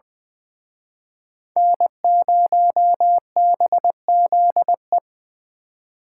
Morse code, heard as a keyed tone, about 24 wpm, 700 Hz. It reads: N0BZE
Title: Speed Racing - USA 1x3 Call Signs 20wpm